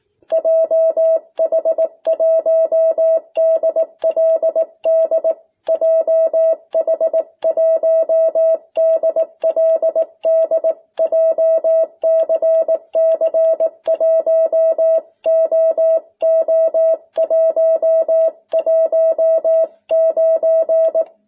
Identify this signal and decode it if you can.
Morse code, heard as a keyed tone, about 18 wpm, 650 Hz. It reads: JH1DLD JH1DLD JCC 1OO119
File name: jh8cbh / 2021cw